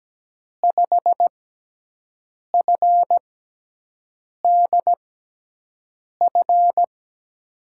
Morse code, heard as a keyed tone, about 17 wpm, spaced as for 6 wpm, 700 Hz. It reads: 5FDF
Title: USA Call Signs - 6 Characters - 6x Character Spacing - 17wpm